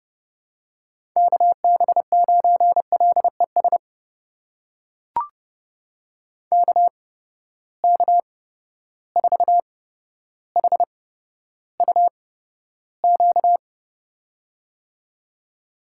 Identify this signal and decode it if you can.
Morse code, heard as a keyed tone, about 30 wpm, spaced as for 13 wpm, 700 Hz. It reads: KB9LES KK4HUQ
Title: USA Call Signs - 6 Characters - 8x Character Spacing - 30wpm